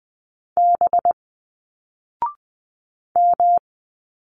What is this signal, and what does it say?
Morse code, heard as a keyed tone, about 20 wpm, 700 Hz. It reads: B M
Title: Single Letters 20wpm